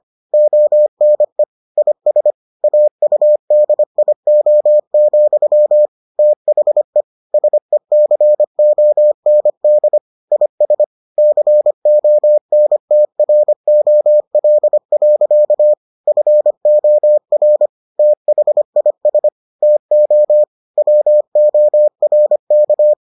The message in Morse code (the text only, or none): ONE IS AUDIO, THE SECOND IS CONTROL. FOR THIS TO WORK